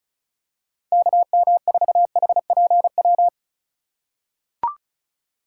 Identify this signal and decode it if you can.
Morse code, heard as a keyed tone, about 36 wpm, 700 Hz. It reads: KM4HPW
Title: USA Call Signs - 6 Characters - 3x Character Spacing - 35wpm